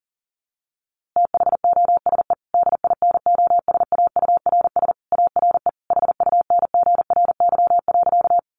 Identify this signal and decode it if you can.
Morse code, heard as a keyed tone, about 40 wpm, 700 Hz. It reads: THOSE DINOSAURS ARE HUNGRY.
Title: A Sound Of Thunder 40wpm